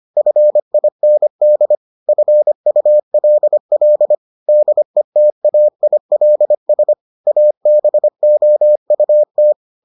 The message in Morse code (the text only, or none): FIND FULL DETAILS ABOUT